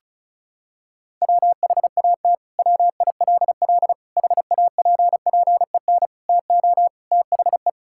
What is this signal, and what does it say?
Morse code, heard as a keyed tone, about 36 wpm, 700 Hz. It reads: WHAT WILL HAPPEN TO THE
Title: Sentences from Top 1000 Words - Review Part 5 - 35wpm